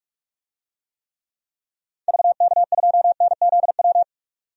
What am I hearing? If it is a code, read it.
Morse code, heard as a keyed tone, about 42 wpm, 700 Hz. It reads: VK2NZW
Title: DXCC-100 Call Signs - 6 Characters - 45wpm